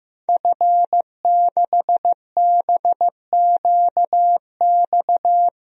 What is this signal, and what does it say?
Morse code, heard as a keyed tone, about 15 wpm, 700 Hz. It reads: F6BQX
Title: DXCC-100 Call Signs - 5 Characters - 15wpm